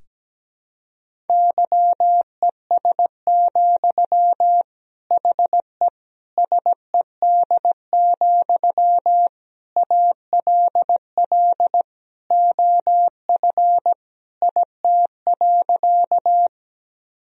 Morse code, heard as a keyed tone, about 17 wpm, 700 Hz. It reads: YES, HE SED, ALL OF IT.
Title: Sentences from Top 500 Words and Abbreviations 17wpm